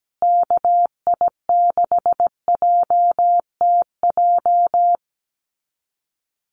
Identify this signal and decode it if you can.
Morse code, heard as a keyed tone, about 17 wpm, 700 Hz. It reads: KI6JTJ
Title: USA Call Signs Repeated 2x 17wpm